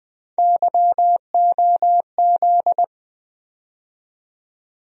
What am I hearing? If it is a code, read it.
Morse code, heard as a keyed tone, about 20 wpm, 700 Hz. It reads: YOZ
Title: USA Call Sign Postfixes - 3 Letters 20wpm